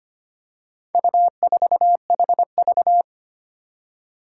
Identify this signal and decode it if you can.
Morse code, heard as a keyed tone, about 25 wpm, 700 Hz. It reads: U4HV
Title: Four Challenging Letter-Number Combinations 25wpm